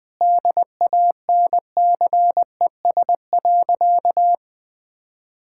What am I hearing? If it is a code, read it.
Morse code, heard as a keyed tone, about 20 wpm, 700 Hz. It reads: DANCES.
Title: Sentences with 3-Word Phrases - Spoken Beginning - 20wpm